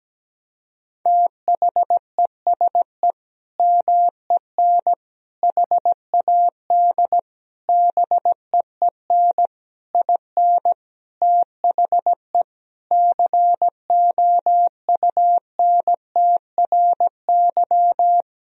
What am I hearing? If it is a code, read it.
Morse code, heard as a keyed tone, about 17 wpm, 700 Hz. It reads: THESE MEN HAD BEEN IN THE COUNTRY
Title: Sentences from Top 1000 Words - Review Part 1 - 17wpm